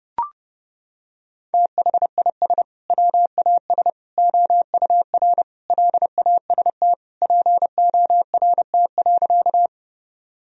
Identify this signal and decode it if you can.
Morse code, heard as a keyed tone, about 32 wpm, 700 Hz. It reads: THIS WAS OUR LAST PORT.
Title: Sentences from Top 1000 Words - Review Part 1 - 30wpm